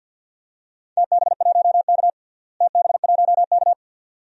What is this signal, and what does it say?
Morse code, heard as a keyed tone, about 52 wpm, 700 Hz. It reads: TB1K TB1K
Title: DXCC-100 Call Signs - 4 Characters - Repeated 2x - 50wpm